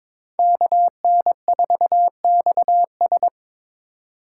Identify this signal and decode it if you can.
Morse code, heard as a keyed tone, about 22 wpm, 700 Hz. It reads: KN4XS
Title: USA Call Signs - 2x Character Spacing - 22wpm